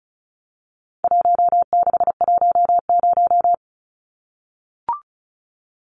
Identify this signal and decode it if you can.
Morse code, heard as a keyed tone, about 34 wpm, 700 Hz. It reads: 1610
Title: Numbers - 4-Digit 35wpm